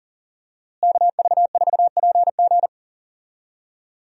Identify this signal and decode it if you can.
Morse code, heard as a keyed tone, about 40 wpm, 700 Hz. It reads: KV4PG
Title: USA Call Signs - 5 Characters - Repeated 3x - 40wpm